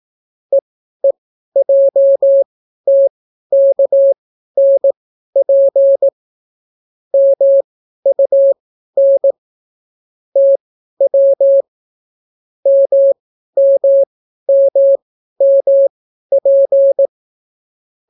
Morse code, heard as a keyed tone, about 18 wpm, spaced as for 12 wpm, 550 Hz. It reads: EEJTKNP MUN TW MMMMP